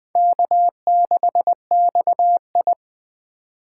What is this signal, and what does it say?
Morse code, heard as a keyed tone, about 21 wpm, 700 Hz. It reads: K6XI